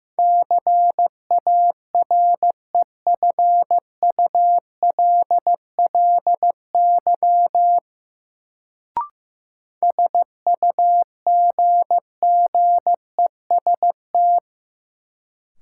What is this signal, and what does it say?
Morse code, heard as a keyed tone, about 15 wpm, 700 Hz. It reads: CAREFULLY SUGGEST
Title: Sentences from Top 1300 Words - New Word Drill - 15wpm